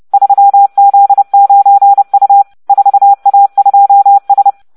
Morse code, heard as a keyed tone, about 30 wpm, 800 Hz. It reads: 3Z9U 4A2S